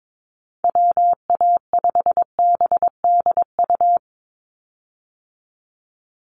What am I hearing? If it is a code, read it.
Morse code, heard as a keyed tone, about 22 wpm, 700 Hz. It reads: WA5BDU
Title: Call Signs Repeated 2x - 22wpm